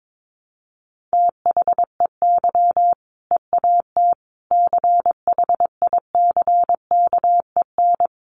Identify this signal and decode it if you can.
Morse code, heard as a keyed tone, about 22 wpm, 700 Hz. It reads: THEY EAT CHICKEN
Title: Easy Phrases 22wpm